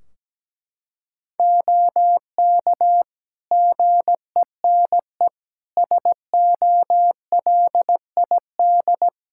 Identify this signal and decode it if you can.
Morse code, heard as a keyed tone, about 17 wpm, 700 Hz. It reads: OK GENE SOLID
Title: Practice QSO - Exchange 3 - More Info - 17wpm